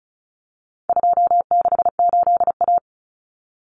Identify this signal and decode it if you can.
Morse code, heard as a keyed tone, about 34 wpm, 700 Hz. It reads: 268A
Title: Common Cut Numbers - 4-Digit 35wpm